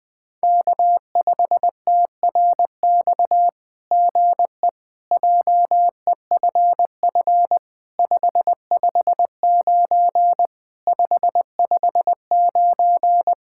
Morse code, heard as a keyed tone, about 19 wpm, 700 Hz. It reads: K5TRX GE JEFF 559 559
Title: SOTA QSO Practice 20wpm